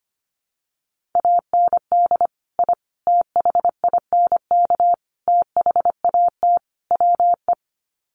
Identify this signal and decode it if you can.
Morse code, heard as a keyed tone, about 25 wpm, 700 Hz. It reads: AND I THINK THAT WE
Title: Sets of 5 Words Encore2 25wpm